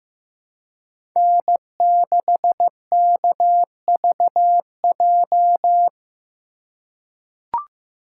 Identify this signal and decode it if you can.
Morse code, heard as a keyed tone, about 15 wpm, 700 Hz. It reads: N6KVJ
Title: USA Call Signs - 5 Characters - 2x Character Spacing - 15wpm